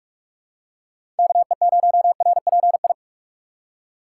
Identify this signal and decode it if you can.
Morse code, heard as a keyed tone, about 46 wpm, 700 Hz. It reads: KE0RPI
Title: USA Call Signs - 3x Character Spacing - 45wpm